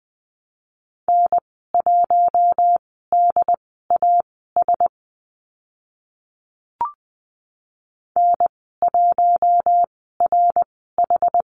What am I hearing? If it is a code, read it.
Morse code, heard as a keyed tone, about 20 wpm, spaced as for 14 wpm, 700 Hz. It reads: N1DAS N1RH
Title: 076 Lesson 13g - Letter L - Call Signs 20wpm